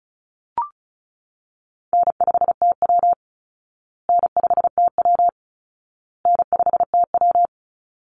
Morse code, heard as a keyed tone, about 34 wpm, 700 Hz. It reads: N5TW N5TW N5TW
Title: Easy Call Signs Repeated 3x - 35wpm